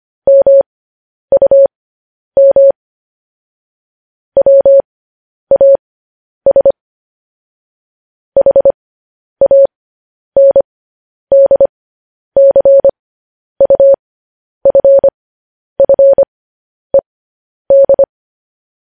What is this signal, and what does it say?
Morse code, heard as a keyed tone, about 25 wpm, spaced as for 10 wpm, 550 Hz. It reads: MUM WAS HANDCUFFED
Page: news headlines in morse code at 10 wpm